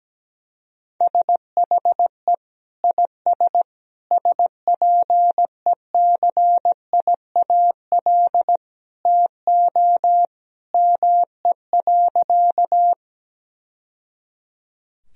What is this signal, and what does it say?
Morse code, heard as a keyed tone, about 17 wpm, 700 Hz. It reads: SHE IS SPECIAL TO ME.
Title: Sentences from Top 1000 Words - Review Part 3 - 17wpm